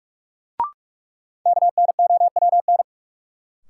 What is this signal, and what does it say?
Morse code, heard as a keyed tone, about 46 wpm, 700 Hz. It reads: KNOWN
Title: Sentences from Top 600 Words - New Word Drill - 45wpm